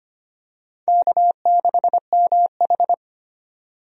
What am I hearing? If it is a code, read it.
Morse code, heard as a keyed tone, about 25 wpm, 700 Hz. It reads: K6MH
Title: USA Call Signs - 4 Characters - 25wpm